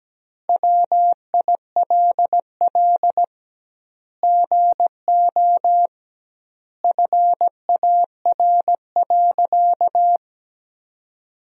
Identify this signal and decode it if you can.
Morse code, heard as a keyed tone, about 17 wpm, 700 Hz. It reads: WILL GO FAR.
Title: 2x Word-Spacing - Sentences from Top 400 Words - 17wpm